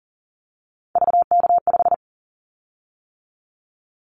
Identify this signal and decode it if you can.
Morse code, heard as a keyed tone, about 41 wpm, 700 Hz. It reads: VK5
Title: Callsign Prefixes for Top 50 DXCC - 40wpm